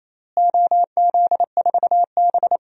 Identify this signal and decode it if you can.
Morse code, heard as a keyed tone, about 28 wpm, 700 Hz. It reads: OZ4B
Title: DXCC-100 Call Signs - 4 Characters - Repeated 2x - 28wpm